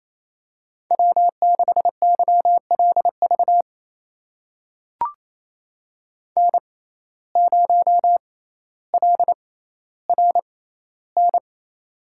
Morse code, heard as a keyed tone, about 28 wpm, spaced as for 14 wpm, 700 Hz. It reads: W6YLV N0LRN
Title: USA Call Signs - 5 Characters - 6x Character Spacing - 28wpm